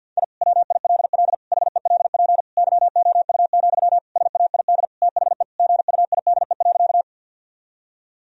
Morse code, heard as a keyed tone, about 50 wpm, 700 Hz. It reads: I WILL HELP YOU, SAID THE GUIDE.
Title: Sentences from Top 1000 Words - Review Part 3 - 50wpm